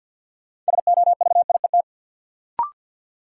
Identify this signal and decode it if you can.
Morse code, heard as a keyed tone, about 50 wpm, 700 Hz. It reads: SOVIET
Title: Sets of 5 Words Encore - New Word Drill - 50wpm